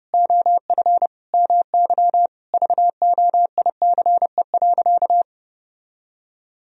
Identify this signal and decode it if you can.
Morse code, heard as a keyed tone, about 32 wpm, 700 Hz. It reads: OF MY VOICE.
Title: Sentences from Top 1000 Words - Review Part 4 - 30wpm